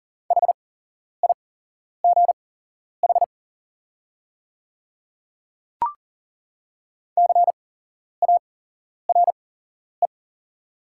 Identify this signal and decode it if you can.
Morse code, heard as a keyed tone, about 42 wpm, spaced as for 11 wpm, 700 Hz. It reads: HIGH CARE
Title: 8x Character Spacing - Top 300 Words - 40wpm